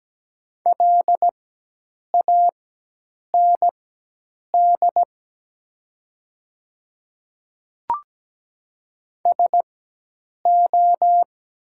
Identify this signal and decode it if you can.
Morse code, heard as a keyed tone, about 17 wpm, spaced as for 8 wpm, 700 Hz. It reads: LAND SO
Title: 4x Character Spacing - Top 200 Words - 17wpm